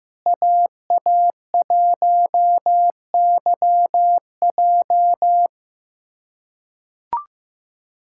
Morse code, heard as a keyed tone, about 15 wpm, 700 Hz. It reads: AA1YJ
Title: USA Callsign DE Callsign - 5 Characters - 15wpm